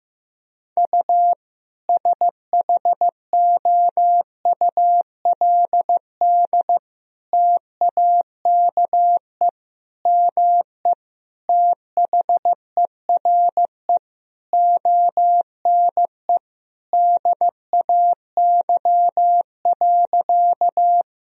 Morse code, heard as a keyed tone, about 15 wpm, 700 Hz. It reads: U SHOULD TAKE ME THERE ONE DAY.
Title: Sentences from Top 300 Words and Abbreviations 15wpm